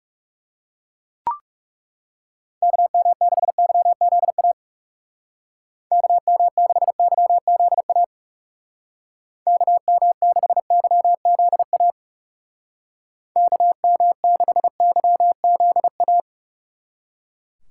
Morse code, heard as a keyed tone, about 37 wpm, 700 Hz. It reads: KM6YZA KM6YZA KM6YZA KM6YZA
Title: Speed Racing - USA Call Signs 30wpm